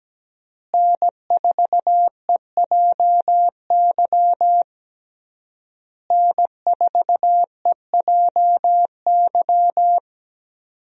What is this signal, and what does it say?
Morse code, heard as a keyed tone, about 17 wpm, 700 Hz. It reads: N4EJY N4EJY
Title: USA Call Signs - 5 Characters - Repeated 3x - 17wpm